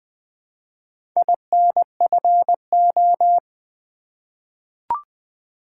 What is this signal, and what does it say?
Morse code, heard as a keyed tone, about 20 wpm, 700 Hz. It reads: INFO
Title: Top 180 QSO Elements 20wpm